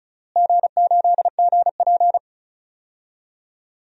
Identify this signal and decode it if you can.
Morse code, heard as a keyed tone, about 36 wpm, 700 Hz. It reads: G8GP
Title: DXCC-100 Call Signs - 4 Characters - Repeated 2x - 35wpm